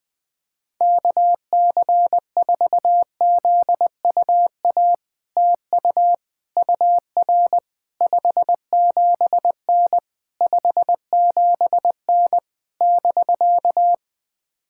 Morse code, heard as a keyed tone, about 20 wpm, 700 Hz. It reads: KC4ZUA TU UR 57N 57N <BK>
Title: POTA QSO Practice - 20wpm